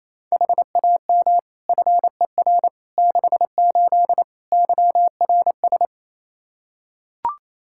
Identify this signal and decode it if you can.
Morse code, heard as a keyed tone, about 28 wpm, 700 Hz. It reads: HAM FER 68 YRS
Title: K1USN SST QSO Practice 28wpm